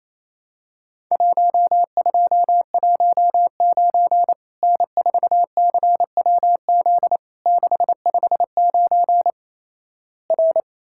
Morse code, heard as a keyed tone, about 28 wpm, 700 Hz. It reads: 1219 N4CWZ 659 R
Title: Worked All Europe DX Contest - QTC Exchange - 28wpm